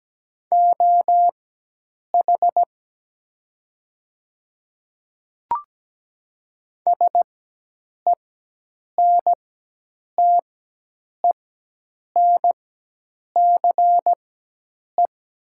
Morse code, heard as a keyed tone, about 17 wpm, spaced as for 8 wpm, 700 Hz. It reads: OH SENTENCE
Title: 4x Character Spacing - Top 500 Words - 17wpm